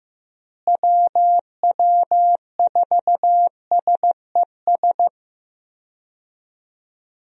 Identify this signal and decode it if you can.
Morse code, heard as a keyed tone, about 15 wpm, 700 Hz. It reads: WW4SES
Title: USA Call Signs - 6 Characters - Repeated 2x - 15wpm